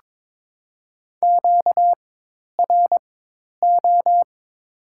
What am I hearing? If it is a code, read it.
Morse code, heard as a keyed tone, about 21 wpm, spaced as for 10 wpm, 700 Hz. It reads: QRO
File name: Top 90 QSO Elements - 4x Character Spacing - 22wpm